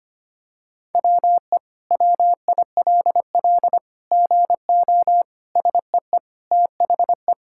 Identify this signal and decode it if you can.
Morse code, heard as a keyed tone, about 25 wpm, 700 Hz. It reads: WE WILL GO SEE THE
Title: Sentences from Top 1000 Words - Review Part 6 - 25wpm